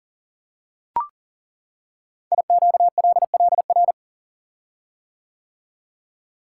Morse code, heard as a keyed tone, about 38 wpm, 690 Hz. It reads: IQLLR
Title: Code Groups - 5 Letters - 40wpm